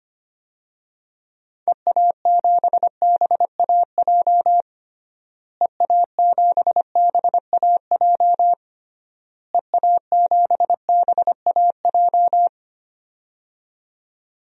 Morse code, heard as a keyed tone, about 25 wpm, 700 Hz. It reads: EA7BAJ EA7BAJ EA7BAJ
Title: DXCC-100 Call Signs - 6 Characters - Repeated 3x - 25wpm